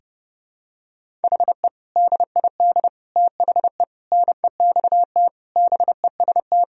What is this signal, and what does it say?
Morse code, heard as a keyed tone, about 30 wpm, 700 Hz. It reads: HE DID THE NEXT BEST
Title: Sentences from Top 1000 Words - Review Part 1 - 30wpm